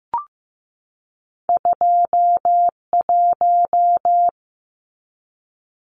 Morse code, heard as a keyed tone, about 15 wpm, 700 Hz. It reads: 21
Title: Numbers - 2-Digit 15wpm